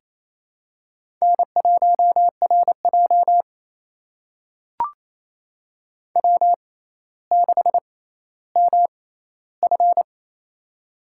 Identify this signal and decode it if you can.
Morse code, heard as a keyed tone, about 28 wpm, spaced as for 14 wpm, 700 Hz. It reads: N1RJ W6MF
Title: USA Call Signs - 4 Characters - 6x Character Spacing - 28wpm